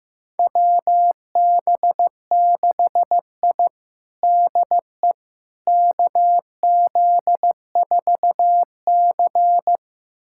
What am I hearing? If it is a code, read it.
Morse code, heard as a keyed tone, about 15 wpm, 700 Hz. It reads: WB6I DE KZ4C
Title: USA Callsign DE Callsign - 4 Characters - 15wpm